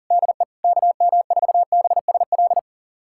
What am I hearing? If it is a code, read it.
Morse code, heard as a keyed tone, about 37 wpm, 700 Hz. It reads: DE KM4BSL
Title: USA Callsign DE Callsign - 6 Characters - 40wpm